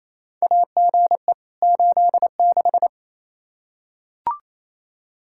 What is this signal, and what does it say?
Morse code, heard as a keyed tone, about 28 wpm, 700 Hz. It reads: AGE 86
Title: K1USN SST QSO Practice 28wpm